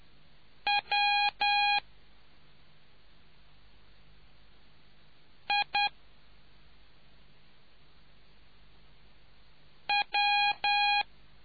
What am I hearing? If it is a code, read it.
Morse code, heard as a keyed tone, about 10 wpm, 820 Hz. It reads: W I W